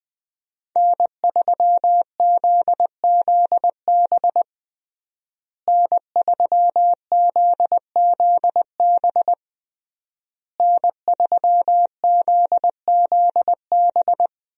Morse code, heard as a keyed tone, about 20 wpm, 700 Hz. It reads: N3ZZB N3ZZB N3ZZB
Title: USA Call Signs - 5 Characters - Repeated 3x - 20wpm